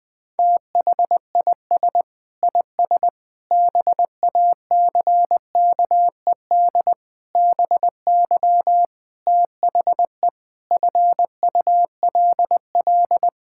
Morse code, heard as a keyed tone, about 20 wpm, 700 Hz. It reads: THIS IS BACKED BY THE FULL
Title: Sentences with 9-Word Phrases - Spoken Beginning - 20wpm